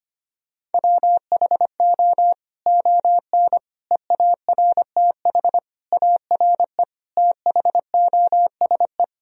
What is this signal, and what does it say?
Morse code, heard as a keyed tone, about 25 wpm, 700 Hz. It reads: WHO ON EARTH ARE THOSE
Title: Sentences from Top 1000 Words - Review Part 1 - 25wpm